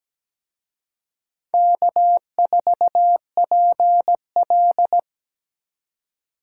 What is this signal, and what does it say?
Morse code, heard as a keyed tone, about 20 wpm, 700 Hz. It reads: K4PL
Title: Speed Racing - USA Call Signs 17wpm